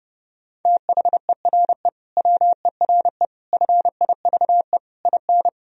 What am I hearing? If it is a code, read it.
Morse code, heard as a keyed tone, about 30 wpm, 700 Hz. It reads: THERE WERE FIVE IN
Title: Sentences from Top 1000 Words - Review Part 2 - 30wpm